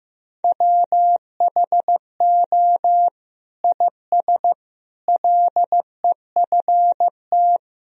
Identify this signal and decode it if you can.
Morse code, heard as a keyed tone, about 15 wpm, 700 Hz. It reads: WHO IS LEFT
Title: Sentences from Top 1000 Words - Review Part 2 - 15wpm